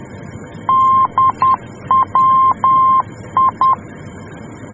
Morse code, heard as a keyed tone, about 10 wpm, 1050 Hz. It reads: DWI
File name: germany / mx / DWI-D